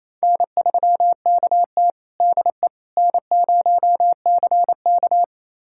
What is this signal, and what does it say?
Morse code, heard as a keyed tone, about 28 wpm, 700 Hz. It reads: N3KT DE N0CK
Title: USA Callsign DE Callsign - 4 Characters - 28wpm